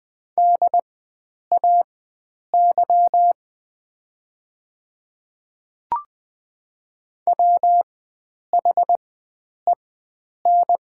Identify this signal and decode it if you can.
Morse code, heard as a keyed tone, about 20 wpm, spaced as for 9 wpm, 700 Hz. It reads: DAY WHEN
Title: 4x Character Spacing - Top 100 Words - 20wpm